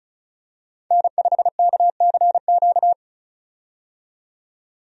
Morse code, heard as a keyed tone, about 34 wpm, 700 Hz. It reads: N5KCQ
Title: USA Call Signs - 5 Characters - 35wpm